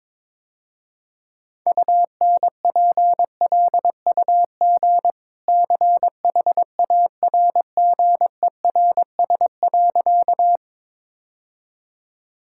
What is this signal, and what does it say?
Morse code, heard as a keyed tone, about 22 wpm, 700 Hz. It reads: UNPLUG CHARGERS.
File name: practice / 22wpm / Easy Sentences with Hints - 22wpm